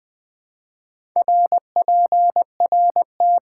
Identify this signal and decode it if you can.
Morse code, heard as a keyed tone, about 20 wpm, 700 Hz. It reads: RPRT
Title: Top 90 QSO Elements - 4x Character Spacing - 20wpm